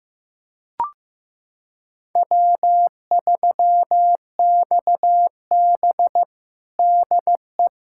Morse code, heard as a keyed tone, about 15 wpm, 700 Hz. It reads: W3XB DE
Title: USA Callsign DE Callsign - 4 Characters - 15wpm